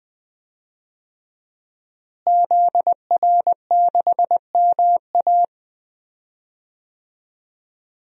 Morse code, heard as a keyed tone, about 20 wpm, 700 Hz. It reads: ZR6MA
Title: DXCC-100 Call Signs - 5 Characters - Repeated 3x - 20wpm